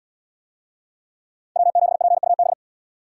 Morse code, heard as a keyed tone, about 70 wpm, 700 Hz. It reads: W7PUB